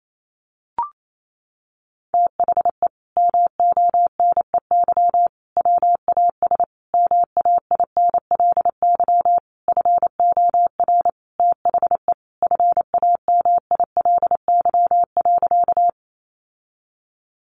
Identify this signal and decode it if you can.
Morse code, heard as a keyed tone, about 28 wpm, 700 Hz. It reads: THE MONEY WAS MAINLY FOR THE FAMILY.
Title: Sentences from Top 1400 Words 28wpm